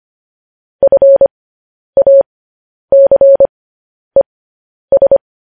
Morse code, heard as a keyed tone, about 25 wpm, spaced as for 10 wpm, 550 Hz. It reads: FACES
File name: podcast / News headlines-1x-25x10